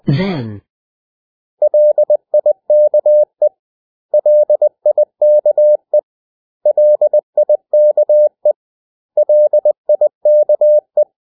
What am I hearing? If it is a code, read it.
Morse code, heard as a keyed tone, about 20 wpm, 600 Hz. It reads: LIKE LIKE LIKE LIKE
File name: cw / 100 Common English Words